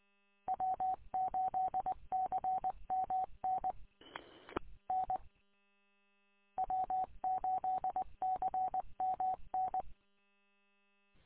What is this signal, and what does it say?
Morse code, heard as a keyed tone, about 20 wpm, 750 Hz. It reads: W8CMN N W8CMN
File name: w8cmn- -combined